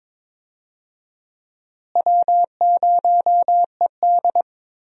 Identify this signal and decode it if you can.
Morse code, heard as a keyed tone, about 22 wpm, 700 Hz. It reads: W0ED
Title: USA Call Signs - 4 Characters - Repeated 3x - 22wpm